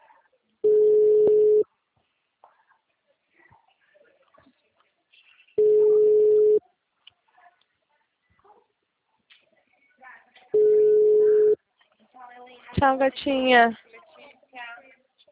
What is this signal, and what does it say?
Ringback tone (Europe, 425 Hz), with speech.